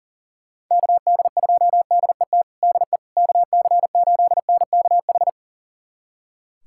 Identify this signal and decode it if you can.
Morse code, heard as a keyed tone, about 40 wpm, 700 Hz. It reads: KD2DET DE KC8NKH
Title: USA Callsign DE Callsign - 6 Characters - 40wpm